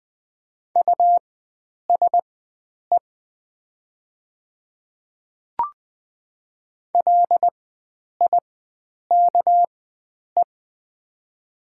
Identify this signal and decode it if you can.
Morse code, heard as a keyed tone, about 20 wpm, spaced as for 9 wpm, 700 Hz. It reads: USE LIKE
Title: 4x Character Spacing - Top 100 Words - 20wpm